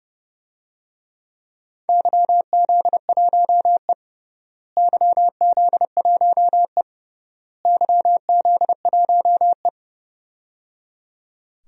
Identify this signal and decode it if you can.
Morse code, heard as a keyed tone, about 30 wpm, 700 Hz. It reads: YZ1E YZ1E YZ1E
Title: DXCC-100 Call Signs - 4 Characters - Repeated 3x - 30wpm